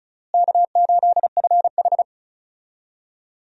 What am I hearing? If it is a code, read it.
Morse code, heard as a keyed tone, about 36 wpm, 700 Hz. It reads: K8FH
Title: USA Call Signs - 4 Characters - Repeated 3x - 35wpm